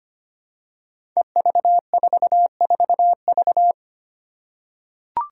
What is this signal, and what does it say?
Morse code, heard as a keyed tone, about 25 wpm, 700 Hz. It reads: EV44V
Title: Five Challenging Letter-Number Combinations 25wpm